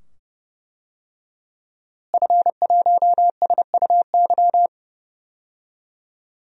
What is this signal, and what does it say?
Morse code, heard as a keyed tone, about 31 wpm, 700 Hz. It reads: F1SUY
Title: DXCC-100 Call Signs - 5 Characters - Repeated 2x - 30wpm